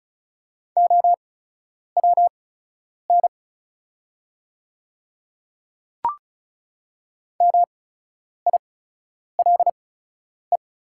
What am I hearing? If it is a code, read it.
Morse code, heard as a keyed tone, about 34 wpm, spaced as for 10 wpm, 700 Hz. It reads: OWN MILE
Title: 8x Character Spacing - Top 300 Words - 35wpm